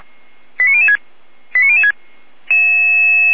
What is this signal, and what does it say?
Five-tone selective call (ZVEI), heard as quick runs of five tones, tones of about 80 ms. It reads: 89006 89006